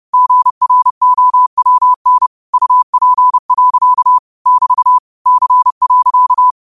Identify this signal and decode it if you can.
Morse code, heard as a keyed tone, about 30 wpm, 1000 Hz. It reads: GROWN UP. <BT> C.
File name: qotdfiles / 30wpm